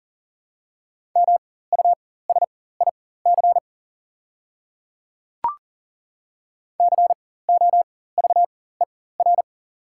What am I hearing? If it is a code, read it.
Morse code, heard as a keyed tone, about 40 wpm, spaced as for 19 wpm, 700 Hz. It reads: MUSIC COVER
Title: 4x Character Spacing - Top 300 Words - 40wpm